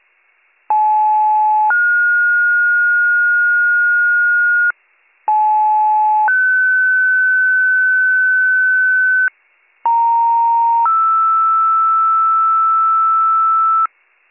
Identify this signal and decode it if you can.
Two-tone page, heard as a one-second tone, then a three-second tone, about 860 then 1430 Hz, about 860 then 1500 Hz, about 930 then 1320 Hz. It is heard from 0.7 to 13.9 s.